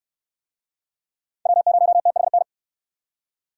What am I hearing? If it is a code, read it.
Morse code, heard as a keyed tone, about 69 wpm, 700 Hz. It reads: W0THN